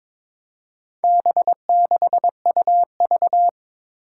Morse code, heard as a keyed tone, about 21 wpm, 700 Hz. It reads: B6UV
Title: Four Challenging Letter-Number Combinations 22wpm